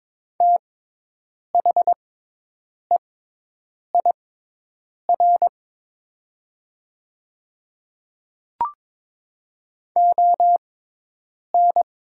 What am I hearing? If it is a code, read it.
Morse code, heard as a keyed tone, about 22 wpm, spaced as for 8 wpm, 700 Hz. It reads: THEIR ON
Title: 6x Character Spacing - Top 100 Words - 22wpm